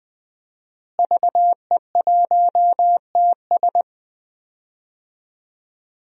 Morse code, heard as a keyed tone, about 20 wpm, 700 Hz. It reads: VE1TS
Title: DXCC-100 Call Signs - 5 Characters - Repeated 3x - 20wpm